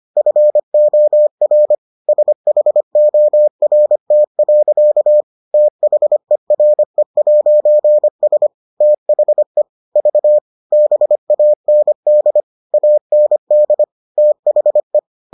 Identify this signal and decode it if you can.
Morse code, heard as a keyed tone, about 25 wpm, 600 Hz. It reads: FOR SHORT. THERE'S THE V BAND AND THE